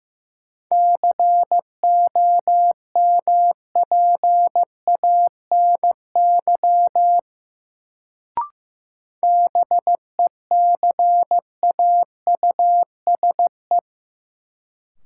Morse code, heard as a keyed tone, about 15 wpm, 700 Hz. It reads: COMPANY BECAUSE
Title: Sentences from Top 600 Words - New Word Drill - 15wpm